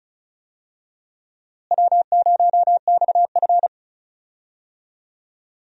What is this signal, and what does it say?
Morse code, heard as a keyed tone, about 34 wpm, 700 Hz. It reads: W0XF
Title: USA Call Signs - 4 Characters - Repeated 3x - 35wpm